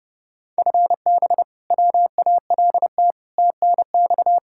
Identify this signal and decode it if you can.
Morse code, heard as a keyed tone, about 32 wpm, 700 Hz. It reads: FB WALT TNX
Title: Practice QSO - Exchange 3 - More Info - 30wpm